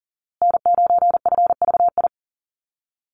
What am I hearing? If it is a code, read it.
Morse code, heard as a keyed tone, about 40 wpm, 700 Hz. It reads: N9FVI
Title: USA Call Signs Repeated 3x 40wpm